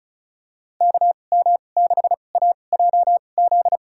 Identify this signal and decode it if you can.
Morse code, heard as a keyed tone, about 34 wpm, spaced as for 25 wpm, 700 Hz. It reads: KM6AJZ
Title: USA Call Signs - 6 Characters - 2x Character Spacing - 35wpm